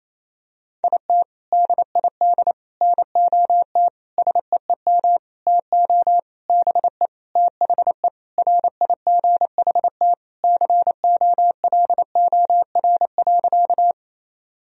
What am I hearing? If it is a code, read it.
Morse code, heard as a keyed tone, about 28 wpm, 700 Hz. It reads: IT DID NOT SEEM TO BE THE RIGHT COLOR.
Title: Sentences from Top 1000 Words - Review Part 2 - 28wpm